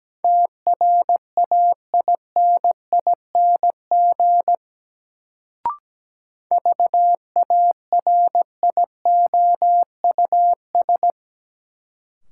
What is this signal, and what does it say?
Morse code, heard as a keyed tone, about 17 wpm, 700 Hz. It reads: TRAINING VARIOUS
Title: Sentences from Top 1000 Words - Review Part 4 - New Word Drill - 17wpm